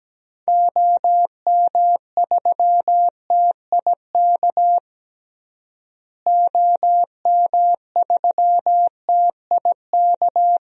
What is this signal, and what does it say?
Morse code, heard as a keyed tone, about 17 wpm, 700 Hz. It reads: OM3TIK OM3TIK
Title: DXCC-100 Call Signs - 6 Characters - Repeated 3x - 17wpm